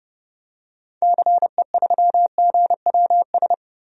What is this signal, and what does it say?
Morse code, heard as a keyed tone, about 30 wpm, 700 Hz. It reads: CE3GWS
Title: DXCC-100 Call Signs - 6 Characters - 30wpm